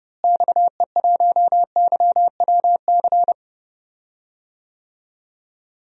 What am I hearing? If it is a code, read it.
Morse code, heard as a keyed tone, about 30 wpm, 700 Hz. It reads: XE1YWC
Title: DXCC-100 Call Signs - 6 Characters - Repeated 3x - 30wpm